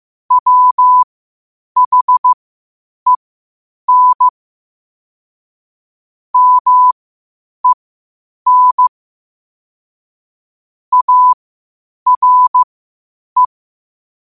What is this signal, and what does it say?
Morse code, heard as a keyed tone, about 15 wpm, spaced as for 8 wpm, 1000 Hz. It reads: WHEN MEN ARE
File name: qotdfiles / 8wpm